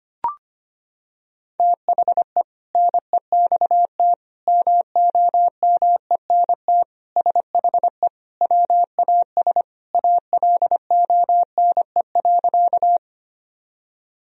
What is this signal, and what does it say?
Morse code, heard as a keyed tone, about 25 wpm, 700 Hz. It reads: THE NEXT MOMENT SHE WAS ALONE.
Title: Sentences from Top 1000 Words - Review Part 1 - 25wpm